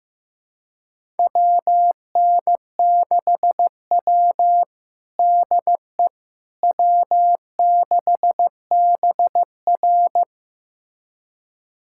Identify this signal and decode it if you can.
Morse code, heard as a keyed tone, about 16 wpm, 700 Hz. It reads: WN6W DE W6BR
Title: USA Callsign DE Callsign - 4 Characters - 15wpm